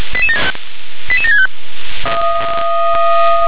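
Five-tone selective call (ZVEI), heard as quick runs of five tones, tones of about 80 ms. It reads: 88765
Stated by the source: ZVEI five-tone call 88775 received by SDR.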